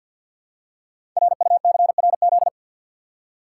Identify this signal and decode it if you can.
Morse code, heard as a keyed tone, about 48 wpm, 690 Hz. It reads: RUCRZ
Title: Code Groups - 5 Letters - 50wpm